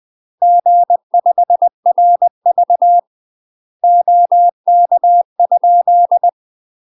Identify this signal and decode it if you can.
Morse code, heard as a keyed tone, about 20 wpm, 700 Hz. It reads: G5RV OK?
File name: UK MASTER QSOs TRAINING / 20WPM / UK MASTER QSOs 0022